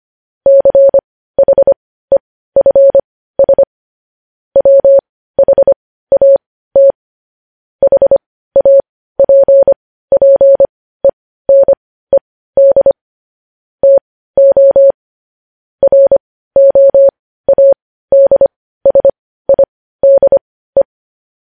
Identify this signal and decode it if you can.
Morse code, heard as a keyed tone, about 25 wpm, spaced as for 15 wpm, 550 Hz. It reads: CHEFS WHAT HAPPENED TO ROADSIDE
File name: podcast / News headlines-1x-25x15